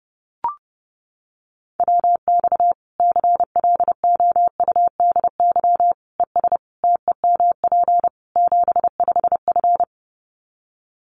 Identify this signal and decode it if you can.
Morse code, heard as a keyed tone, about 30 wpm, 700 Hz. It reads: WX CLOUDY ES TEMP 75F
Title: Common QSO Phrases 30wpm